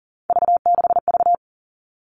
Morse code, heard as a keyed tone, about 40 wpm, 700 Hz. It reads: V6V
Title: All Cut Numbers - 3-Digit 40wpm